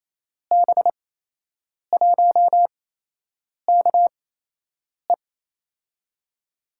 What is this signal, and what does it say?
Morse code, heard as a keyed tone, about 28 wpm, spaced as for 8 wpm, 700 Hz. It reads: B1KE
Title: USA Call Signs - 5 Characters - 8x Character Spacing - 28wpm